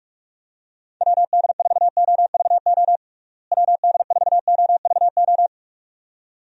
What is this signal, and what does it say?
Morse code, heard as a keyed tone, about 46 wpm, 700 Hz. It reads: WD4OVO WD4OVO
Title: USA Call Signs - 6 Characters - Repeated 2x - 45wpm